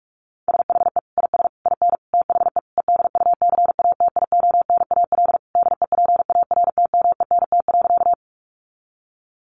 Morse code, heard as a keyed tone, about 44 wpm, 700 Hz. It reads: SHE IS IN THE EDUCATIONAL DEPARTMENT.
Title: Sentences from Top 1200 Words 45wpm